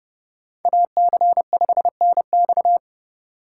Morse code, heard as a keyed tone, about 28 wpm, 700 Hz. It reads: AC5NX